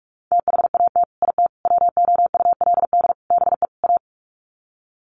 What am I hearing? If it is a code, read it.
Morse code, heard as a keyed tone, about 44 wpm, 700 Hz. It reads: THAT IT WOULD BE A
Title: Sets of 5 Words Encore2 45wpm